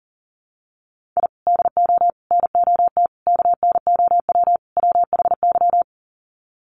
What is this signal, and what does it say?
Morse code, heard as a keyed tone, about 40 wpm, 700 Hz. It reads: I DO NOT KNOW WHY
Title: Sets of 5 Words 40wpm